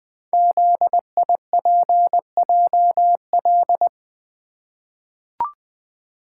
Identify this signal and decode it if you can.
Morse code, heard as a keyed tone, about 21 wpm, 700 Hz. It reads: ZIPJL
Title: Code Groups - 5 Letter:Numbers - 20wpm